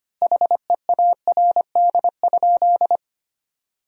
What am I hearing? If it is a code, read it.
Morse code, heard as a keyed tone, about 25 wpm, 700 Hz. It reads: HEARD?
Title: 2x Word-Spacing - Sentences from Top 400 Words - 25wpm